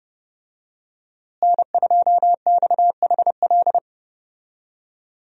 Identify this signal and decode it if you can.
Morse code, heard as a keyed tone, about 30 wpm, 700 Hz. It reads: N2XHL
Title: USA Call Signs - 5 Characters - Repeated 2x - 30wpm